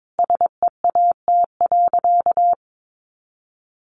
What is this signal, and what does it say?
Morse code, heard as a keyed tone, about 22 wpm, 700 Hz. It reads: SEAT.
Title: Sentences from Top 1500 Words 22wpm